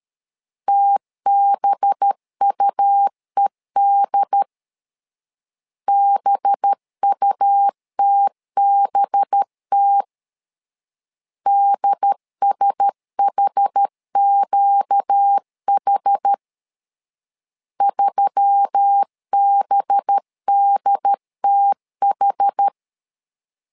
Morse code, heard as a keyed tone, about 13 wpm, 800 Hz. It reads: TBUED BUTBT DSHQH 3BDTH
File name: LEZIONI / 5 LBVC / 5 LB 12wpm 1906